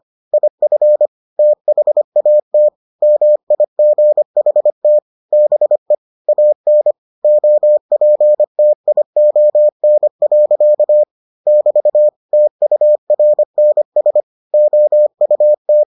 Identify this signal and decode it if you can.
Morse code, heard as a keyed tone, about 25 wpm, 600 Hz. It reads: IF THAT MIGHT BE AN OPTION. <BT> TURNS OUT